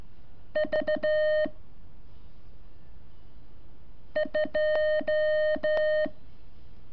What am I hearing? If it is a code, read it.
Morse code, heard as a keyed tone, about 12 wpm, 610 Hz. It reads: V 2
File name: curso / ejercicio 24